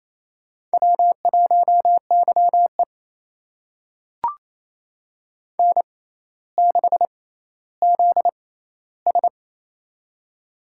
Morse code, heard as a keyed tone, about 28 wpm, spaced as for 14 wpm, 700 Hz. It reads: W1YE N6ZS
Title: USA Call Signs - 4 Characters - 6x Character Spacing - 28wpm